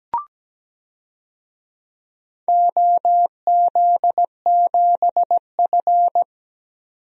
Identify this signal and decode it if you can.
Morse code, heard as a keyed tone, about 17 wpm, 700 Hz. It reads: OZ7F
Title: DXCC-100 Call Signs - 4 Characters - 17wpm